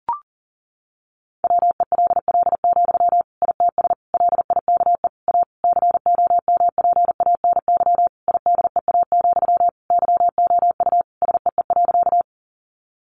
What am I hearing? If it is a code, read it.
Morse code, heard as a keyed tone, about 40 wpm, 700 Hz. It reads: WELL, ITS LIKE A COMPANY IDEA, YOU SEE.
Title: Sentences from Top 600 Words 40wpm